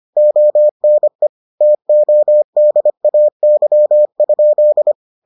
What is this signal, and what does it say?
Morse code, heard as a keyed tone, about 25 wpm, 600 Hz. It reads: ONE TODAY?